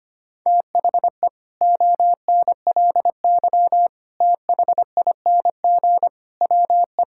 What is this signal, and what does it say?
Morse code, heard as a keyed tone, about 25 wpm, 700 Hz. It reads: THE ONLY THING WE
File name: practice / 25wpm / Sentences with 9-Word Phrases - 25wpm